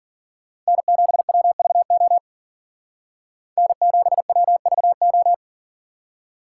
Morse code, heard as a keyed tone, about 48 wpm, 700 Hz. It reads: N7WVO N7WVO
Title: Speed Racing - USA 1x3 Call Signs 40wpm